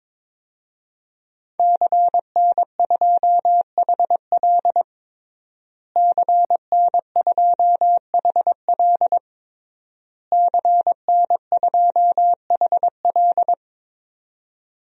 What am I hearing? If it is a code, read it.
Morse code, heard as a keyed tone, about 22 wpm, 700 Hz. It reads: CN2HL CN2HL CN2HL
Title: DXCC-100 Call Signs - 5 Characters - Repeated 3x - 22wpm